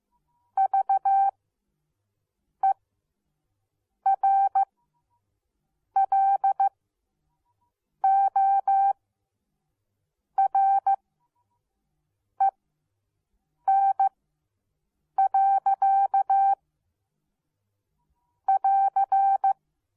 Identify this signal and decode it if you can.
Morse code, heard as a keyed tone, about 15 wpm, spaced as for 6 wpm, 780 Hz. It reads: VERLOREN. <AR>